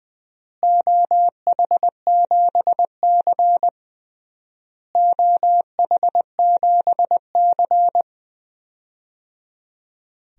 Morse code, heard as a keyed tone, about 20 wpm, 700 Hz. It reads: OH7C OH7C
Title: DXCC-100 Call Signs - 4 Characters - Repeated 3x - 20wpm